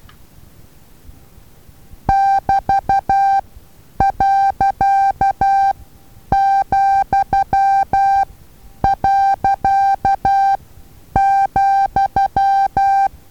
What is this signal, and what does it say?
Morse code, heard as a keyed tone, about 12 wpm, spaced as for 9 wpm, 790 Hz. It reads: <BT>.,.,
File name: Taviro HA6VA A